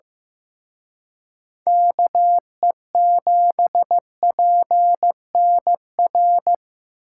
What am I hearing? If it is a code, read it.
Morse code, heard as a keyed tone, about 18 wpm, 700 Hz. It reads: KE7PNR
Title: Speed Racing - USA 2x3 Call Signs 15wpm